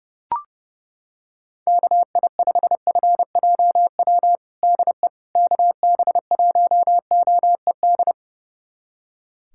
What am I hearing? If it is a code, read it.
Morse code, heard as a keyed tone, about 31 wpm, 700 Hz. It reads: KI5FJW DE KB1OED
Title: USA Callsign DE Callsign - 6 Characters - 30wpm